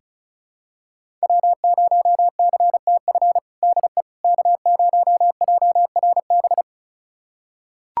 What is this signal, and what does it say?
Morse code, heard as a keyed tone, about 34 wpm, 700 Hz. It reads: W0CTF DE K0JRB
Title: USA Callsign DE Callsign - 5 Characters - 35wpm